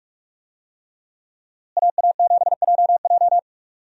Morse code, heard as a keyed tone, about 41 wpm, 700 Hz. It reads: AA7JJ
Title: USA Call Signs - 5 Characters - Repeated 3x - 45wpm